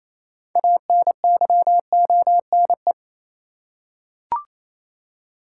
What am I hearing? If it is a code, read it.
Morse code, heard as a keyed tone, about 28 wpm, 700 Hz. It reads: ANYONE
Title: Six Letter Words - 28wpm